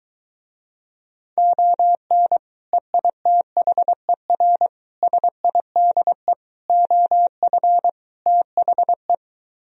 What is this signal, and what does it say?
Morse code, heard as a keyed tone, about 23 wpm, 700 Hz. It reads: ON EITHER SIDE OF THE